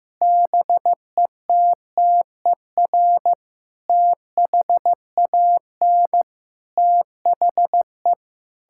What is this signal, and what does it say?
Morse code, heard as a keyed tone, about 16 wpm, 700 Hz. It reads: BETTER THAN THE
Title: Sentences from Top 1000 Words - Review Part 6 - 15wpm